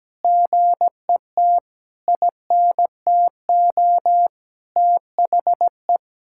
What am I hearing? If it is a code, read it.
Morse code, heard as a keyed tone, about 17 wpm, 700 Hz. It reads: GET INTO THE
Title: Sentences from Top 1600 Words 17wpm